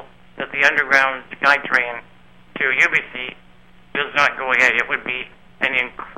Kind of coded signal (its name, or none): Emergency Alert System (SAME) tones